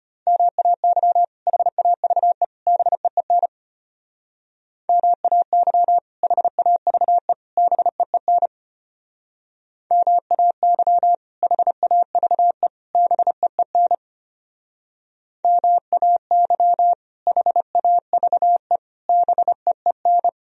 Morse code, heard as a keyed tone, about 33 wpm, 700 Hz. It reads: MAY HAVE BEEN MAY HAVE BEEN MAY HAVE BEEN MAY HAVE BEEN